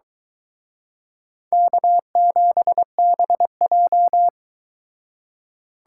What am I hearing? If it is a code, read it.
Morse code, heard as a keyed tone, about 23 wpm, 700 Hz. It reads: K7BJ